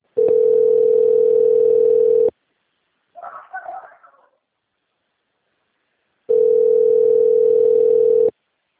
Ringback tone (North America), 440 + 480 Hz.